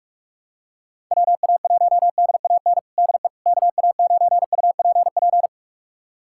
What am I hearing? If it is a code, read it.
Morse code, heard as a keyed tone, about 42 wpm, 700 Hz. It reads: WA1DAN DE KA9UPP
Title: USA Callsign DE Callsign 45wpm